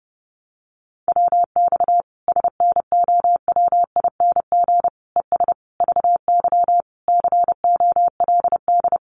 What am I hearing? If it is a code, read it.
Morse code, heard as a keyed tone, about 30 wpm, 700 Hz. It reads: WX SNOWING ES VY COLD
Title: Common QSO Phrases 30wpm